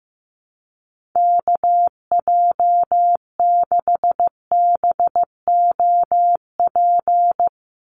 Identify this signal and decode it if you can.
Morse code, heard as a keyed tone, about 15 wpm, 700 Hz. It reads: KJ6BOP
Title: USA Call Signs 15wpm